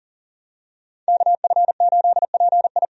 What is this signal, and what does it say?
Morse code, heard as a keyed tone, about 40 wpm, 700 Hz. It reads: KF8PI
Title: USA Call Signs - 5 Characters - 40wpm